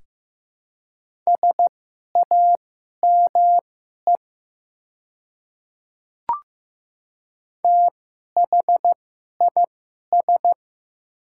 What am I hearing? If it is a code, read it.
Morse code, heard as a keyed tone, about 15 wpm, spaced as for 11 wpm, 700 Hz. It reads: SAME THIS
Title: 2x Character Spacing - Top 300 Words - 15wpm